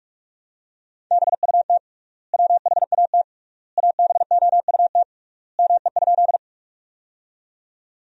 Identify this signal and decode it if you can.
Morse code, heard as a keyed tone, about 46 wpm, 700 Hz. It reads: BUT WHAT ABOUT ME?
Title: 3x Word-Spacing - Sentences from Top 200 Words - 45wpm